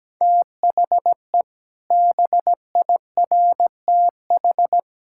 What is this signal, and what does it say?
Morse code, heard as a keyed tone, about 17 wpm, 700 Hz. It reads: THE BIRTH
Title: Sentences from Top 1000 Words - Review Part 5 - 17wpm